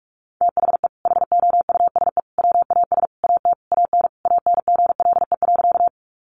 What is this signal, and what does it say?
Morse code, heard as a keyed tone, about 44 wpm, 700 Hz. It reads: THE HOUSE WAS AT AN ANGLE.
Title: Sentences from Top 1500 Words 45wpm